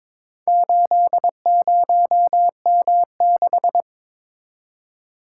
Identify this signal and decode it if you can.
Morse code, heard as a keyed tone, about 22 wpm, 700 Hz. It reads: 80M6
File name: Four Letter-Number Combinations - 22wpm